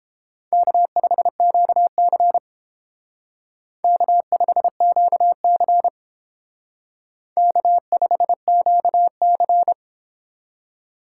Morse code, heard as a keyed tone, about 30 wpm, 700 Hz. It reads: K5QC K5QC K5QC